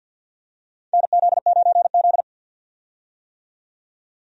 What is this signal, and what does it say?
Morse code, heard as a keyed tone, about 51 wpm, 700 Hz. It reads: NZ9Z
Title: USA Call Signs - 4 Characters - 50wpm